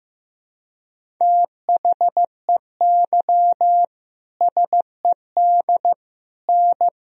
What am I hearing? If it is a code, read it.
Morse code, heard as a keyed tone, about 16 wpm, 700 Hz. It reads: THEY SED N
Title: Sentences from Top 100 Words and Abbreviations 15wpm